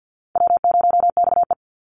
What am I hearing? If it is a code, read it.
Morse code, heard as a keyed tone, about 50 wpm, 700 Hz. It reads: W0XE
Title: Easy Call Signs - 50wpm